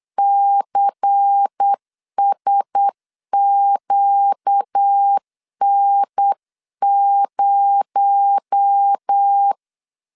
Morse code, heard as a keyed tone, about 8 wpm, 800 Hz. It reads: CSQN0
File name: LEZIONI / 10 SIGNS / 10D COMA 8wpm 3749